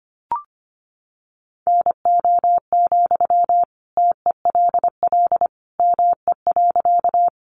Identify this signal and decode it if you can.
Morse code, heard as a keyed tone, about 25 wpm, 700 Hz. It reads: NO, TELL ME.
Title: Sentences from Top 200 Words 25wpm